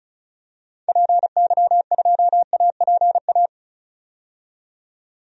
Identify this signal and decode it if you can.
Morse code, heard as a keyed tone, about 34 wpm, 700 Hz. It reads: PY2APA
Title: DXCC-100 Call Signs - 6 Characters - 35wpm